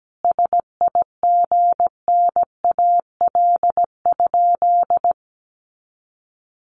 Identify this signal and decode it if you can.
Morse code, heard as a keyed tone, about 17 wpm, 700 Hz. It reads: SIGNAL?
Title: Sentences from Top 1400 Words 17wpm